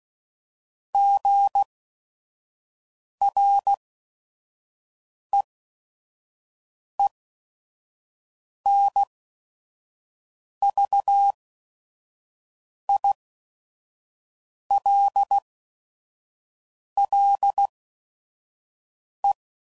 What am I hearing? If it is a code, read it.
Morse code, heard as a keyed tone, about 16 wpm, spaced as for 5 wpm, 800 Hz. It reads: GREENVILLE
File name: CW Code Groups @5WPM / fromSM's / Cities